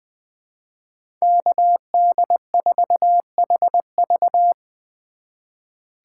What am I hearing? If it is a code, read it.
Morse code, heard as a keyed tone, about 20 wpm, 700 Hz. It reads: KD4HV
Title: USA Call Signs - 5 Characters - 2x Character Spacing - 20wpm